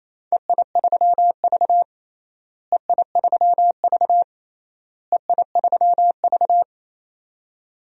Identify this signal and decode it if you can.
Morse code, heard as a keyed tone, about 28 wpm, 700 Hz. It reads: EI3V EI3V EI3V
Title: DXCC-100 Call Signs - 4 Characters - Repeated 3x - 28wpm